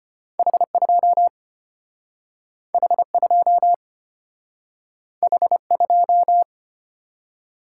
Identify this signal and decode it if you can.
Morse code, heard as a keyed tone, about 31 wpm, 700 Hz. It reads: H2 H2 H2